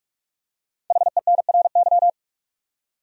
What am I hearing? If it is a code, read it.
Morse code, heard as a keyed tone, about 46 wpm, 700 Hz. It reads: HENRY
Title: USA Common Names - Max 5 Letters - 45wpm